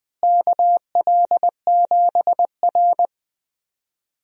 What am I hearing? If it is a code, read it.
Morse code, heard as a keyed tone, about 20 wpm, 700 Hz. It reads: KL7R
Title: DXCC-100 Call Signs - 4 Characters - Repeated 3x - 20wpm